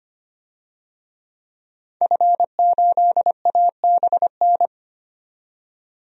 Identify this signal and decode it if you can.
Morse code, heard as a keyed tone, about 25 wpm, 700 Hz. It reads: F8ABN
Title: DXCC-100 Call Signs - 5 Characters - 25wpm